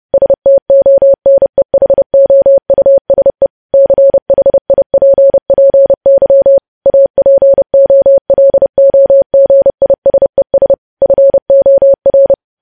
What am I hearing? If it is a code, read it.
Morse code, heard as a keyed tone, about 30 wpm, 550 Hz. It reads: STONEHOUSE CHIPPY APOLOGISES FOR
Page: news headlines in morse code at 30 wpm